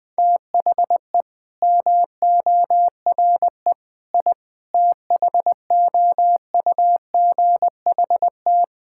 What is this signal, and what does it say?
Morse code, heard as a keyed tone, about 20 wpm, 700 Hz. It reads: THE MORE I THOUGHT
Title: Sentences with 9-Word Phrases - 20wpm